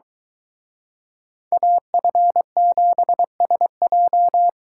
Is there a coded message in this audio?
Yes: Morse code from 1.5 to 4.5 s, heard as a keyed tone, about 24 wpm, 700 Hz. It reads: AF7SJ